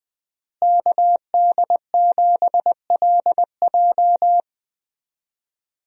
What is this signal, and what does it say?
Morse code, heard as a keyed tone, about 20 wpm, 700 Hz. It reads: KD7LJ
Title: USA Call Signs - 5 Characters - Repeated 3x - 20wpm